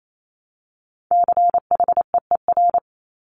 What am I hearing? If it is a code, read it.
Morse code, heard as a keyed tone, about 28 wpm, 700 Hz. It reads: CHEER
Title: Multi-letter Phonemes Repeated 3x 28wpm